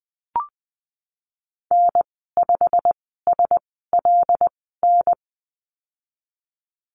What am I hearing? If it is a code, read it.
Morse code, heard as a keyed tone, about 20 wpm, spaced as for 15 wpm, 700 Hz. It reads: N5SLN
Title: 154 Lesson 25g - Letter p - Call Signs 20wpm